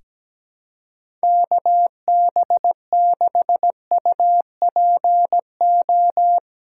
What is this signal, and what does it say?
Morse code, heard as a keyed tone, about 17 wpm, 700 Hz. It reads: KB6UPO
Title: USA Call Signs - 6 Characters - 8x Character Spacing - 17wpm